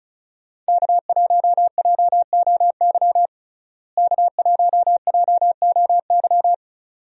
Morse code, heard as a keyed tone, about 34 wpm, 700 Hz. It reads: K1JOY K1JOY
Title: USA Call Signs - 5 Characters - Repeated 2x - 35wpm